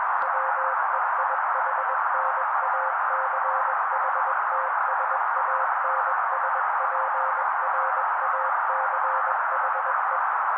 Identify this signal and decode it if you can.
Morse code, heard as a keyed tone, about 21 wpm, 530 Hz. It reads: WEIHNACHTSANSPRACHE
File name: daily / 20wpm noise